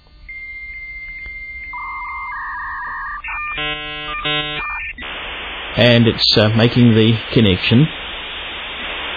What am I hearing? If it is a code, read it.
Dial-up modem handshake, with speech.